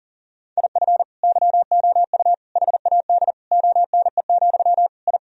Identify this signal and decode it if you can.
Morse code, heard as a keyed tone, about 37 wpm, 700 Hz. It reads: IF YOU HAD ONE, I
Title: Sentences from Top 1000 Words - Review Part 3 - 40wpm